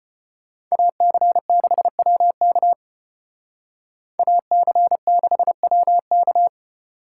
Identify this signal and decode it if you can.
Morse code, heard as a keyed tone, about 31 wpm, 700 Hz. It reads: AC6WK AC6WK